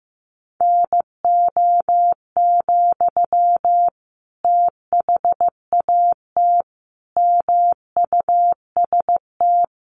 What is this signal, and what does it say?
Morse code, heard as a keyed tone, about 15 wpm, 700 Hz. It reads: NO, THAT MUST
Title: Sentences from Top 600 Words 15wpm